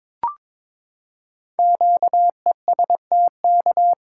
Morse code, heard as a keyed tone, about 22 wpm, 700 Hz. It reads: QESTK
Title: Code Groups - 5 Letters - 22wpm